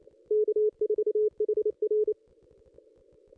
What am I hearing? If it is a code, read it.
Morse code, heard as a keyed tone, about 29 wpm, 440 Hz. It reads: K4HR
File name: contests / ss-cw-k5tr-op-n5rz -1900